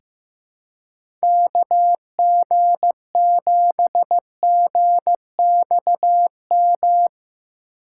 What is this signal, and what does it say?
Morse code, heard as a keyed tone, about 15 wpm, 700 Hz. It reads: KG7GXM
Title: USA Call Signs - 6 Characters - 2x Character Spacing - 15wpm